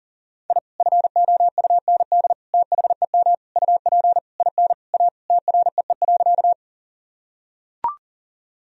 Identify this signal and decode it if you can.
Morse code, heard as a keyed tone, about 40 wpm, 700 Hz. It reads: I FOUND THEM UP IN A TREE.
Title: Sentences from Top 1000 Words - Review Part 1 - 40wpm